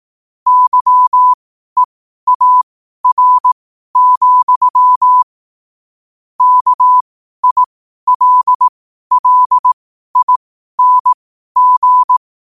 Morse code, heard as a keyed tone, about 18 wpm, spaced as for 12 wpm, 1000 Hz. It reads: YEAR, KILLING
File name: qotdfiles / 12wpm